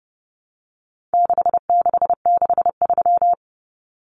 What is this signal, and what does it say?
Morse code, heard as a keyed tone, about 30 wpm, 700 Hz. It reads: 6663
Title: Numbers - 4-Digit 30wpm